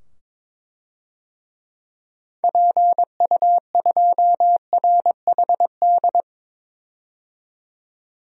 Morse code, heard as a keyed tone, about 21 wpm, 700 Hz. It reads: PU2RHD
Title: DXCC-100 Call Signs - 6 Characters - 22wpm